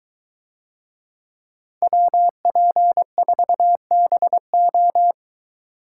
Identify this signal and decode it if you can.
Morse code, heard as a keyed tone, about 21 wpm, 700 Hz. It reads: WP4BO